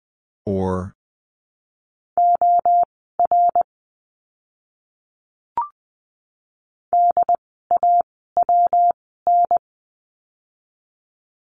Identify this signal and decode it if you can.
Morse code, heard as a keyed tone, about 20 wpm, spaced as for 14 wpm, 700 Hz. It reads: OR DAWN
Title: 115 Lesson 19f - Letter w - Words 20wpm